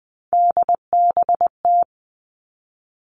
Morse code, heard as a keyed tone, about 20 wpm, 700 Hz. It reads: DBT
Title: Three Challenging Letter-Number Combinations 20wpm